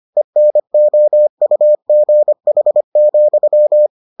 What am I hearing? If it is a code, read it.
Morse code, heard as a keyed tone, about 25 wpm, 600 Hz. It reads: ENOUGH,